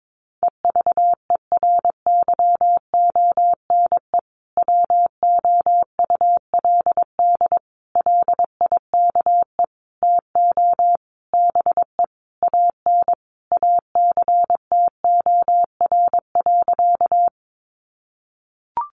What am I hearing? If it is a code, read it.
Morse code, heard as a keyed tone, about 22 wpm, 700 Hz. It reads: EVERYONE WOULD LIKE TO BE AN ACTOR.
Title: Sentences from Top 1500 Words 22wpm